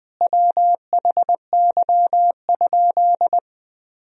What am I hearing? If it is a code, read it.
Morse code, heard as a keyed tone, about 20 wpm, 700 Hz. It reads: WHY?
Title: Sentences from Top 200 Words and Abbreviations 20wpm